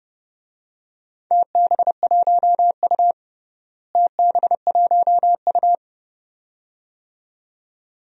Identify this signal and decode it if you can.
Morse code, heard as a keyed tone, about 30 wpm, 700 Hz. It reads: TB1U TB1U
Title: DXCC-100 Call Signs - 4 Characters - Repeated 2x - 30wpm